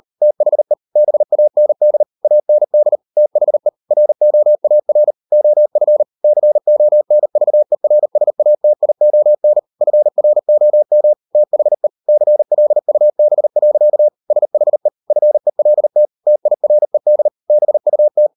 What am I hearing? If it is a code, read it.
Morse code, heard as a keyed tone, about 38 wpm, 600 Hz. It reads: THE BAND AND THE ROAR OF CONVERSATION FROM THE CLUB. SHE FELT TIRED BUT